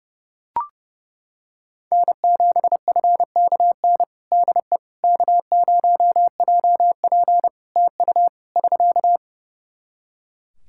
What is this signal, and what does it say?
Morse code, heard as a keyed tone, about 30 wpm, 700 Hz. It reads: N7FKN DE K0JP TU <SK>
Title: Practice QSO - Exchange 4 - Ending - 30wpm